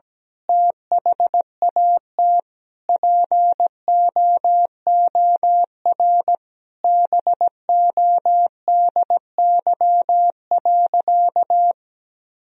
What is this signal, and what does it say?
Morse code, heard as a keyed tone, about 17 wpm, 700 Hz. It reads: THAT POOR BODY.
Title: Sentences from Top 1000 Words - Review Part 2 - 17wpm